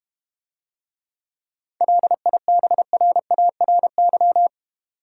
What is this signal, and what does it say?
Morse code, heard as a keyed tone, about 32 wpm, 700 Hz. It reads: LIBRARY